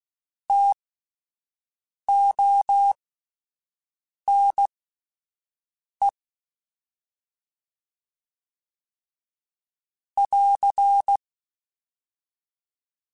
Morse code, heard as a keyed tone, about 16 wpm, spaced as for 5 wpm, 800 Hz. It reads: TONE <AR>
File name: Alphabetical / Alphabetical mp3s / Ts @5